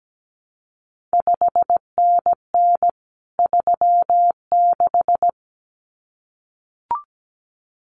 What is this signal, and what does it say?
Morse code, heard as a keyed tone, about 17 wpm, 700 Hz. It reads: 5NN 36
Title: CQ WW DX Contest Exchange Encore - 17wpm